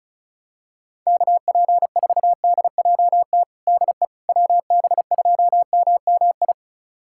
Morse code, heard as a keyed tone, about 36 wpm, 700 Hz. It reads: KP4DJT DE WB2MMI
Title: USA Callsign DE Callsign - 6 Characters - 35wpm